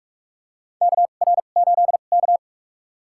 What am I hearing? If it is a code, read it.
Morse code, heard as a keyed tone, about 46 wpm, spaced as for 33 wpm, 700 Hz. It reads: KR8K
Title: USA Call Signs - 4 Characters - 2x Character Spacing - 45wpm